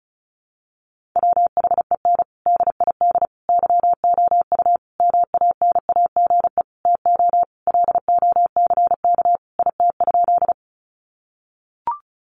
Morse code, heard as a keyed tone, about 34 wpm, 700 Hz. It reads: WHEN DID YOU MANAGE TO LOCK IT?
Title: Sentences from Top 1400 Words 35wpm